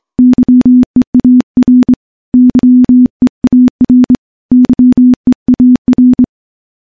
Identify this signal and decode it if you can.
Morse code, heard as a keyed tone, about 24 wpm, 250 Hz. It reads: YEAR YEAR YEAR
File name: morse-training-25WPM